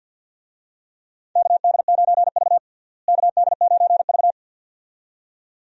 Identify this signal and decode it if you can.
Morse code, heard as a keyed tone, about 52 wpm, 700 Hz. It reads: KD9V KD9V
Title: USA Call Signs - 4 Characters - Repeated 2x - 50wpm